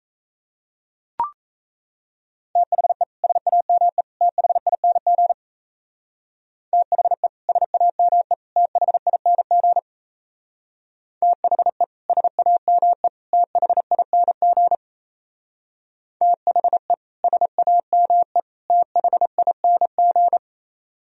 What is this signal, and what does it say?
Morse code, heard as a keyed tone, about 36 wpm, 700 Hz. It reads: THE SAME THING THE SAME THING THE SAME THING THE SAME THING